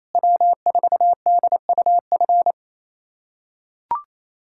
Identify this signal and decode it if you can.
Morse code, heard as a keyed tone, about 28 wpm, 700 Hz. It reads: W4DUF
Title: USA Callsign DE Callsign - 5 Characters - 28wpm